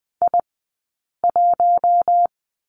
Morse code, heard as a keyed tone, about 20 wpm, 700 Hz. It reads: I 1
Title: 190 Lesson 31d - Letter v - Two Characters 20wpm